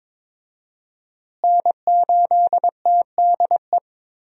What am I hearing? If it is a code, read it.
Morse code, heard as a keyed tone, about 22 wpm, 700 Hz. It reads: N8TDE
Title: USA Call Signs - 5 Characters - Repeated 3x - 22wpm